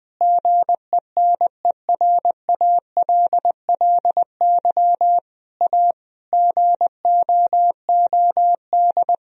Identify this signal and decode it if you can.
Morse code, heard as a keyed tone, about 20 wpm, 700 Hz. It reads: GENERALLY A GOOD
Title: Sentences from Top 1000 Words - Review Part 4 - 20wpm